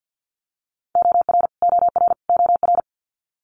Morse code, heard as a keyed tone, about 50 wpm, 700 Hz. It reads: OR OR OR
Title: Multi-letter Phonemes Repeated 3x 50wpm